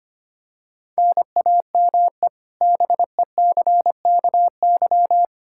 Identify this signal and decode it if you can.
Morse code, heard as a keyed tone, about 25 wpm, 700 Hz. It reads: NAME BECKY
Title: K1USN SST QSO Practice 25wpm